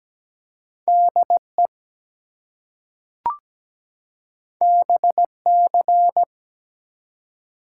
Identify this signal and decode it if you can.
Morse code, heard as a keyed tone, about 17 wpm, 700 Hz. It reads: DE BC
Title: US States and Canadian Provinces 17wpm